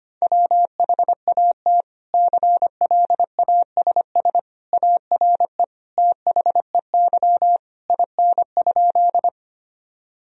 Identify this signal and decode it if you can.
Morse code, heard as a keyed tone, about 25 wpm, 700 Hz. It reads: WHAT CLASS ARE THEY IN?
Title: Sentences from Top 1000 Words - Review Part 2 - 25wpm